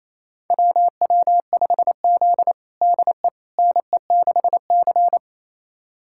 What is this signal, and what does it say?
Morse code, heard as a keyed tone, about 28 wpm, 700 Hz. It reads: WW5Z DE NE6C
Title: USA Callsign DE Callsign - 4 Characters - 28wpm